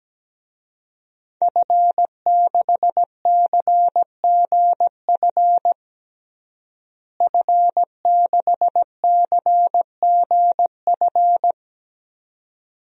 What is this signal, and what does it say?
Morse code, heard as a keyed tone, about 17 wpm, 700 Hz. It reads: F6CGF F6CGF
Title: DXCC-100 Call Signs - 5 Characters - Repeated 2x - 17wpm